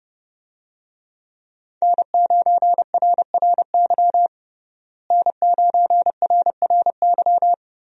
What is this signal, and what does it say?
Morse code, heard as a keyed tone, about 30 wpm, 700 Hz. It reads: N9RRY N9RRY
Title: USA Call Signs - 5 Characters - Repeated 2x - 30wpm